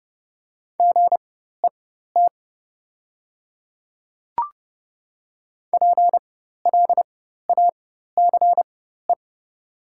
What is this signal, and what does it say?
Morse code, heard as a keyed tone, about 30 wpm, spaced as for 14 wpm, 690 Hz. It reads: GET PLACE
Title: 4x Character Spacing - Top 200 Words - 30wpm